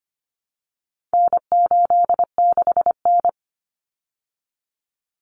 Morse code, heard as a keyed tone, about 25 wpm, 700 Hz. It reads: N86N
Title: Common Cut Numbers - 4-Digit 25wpm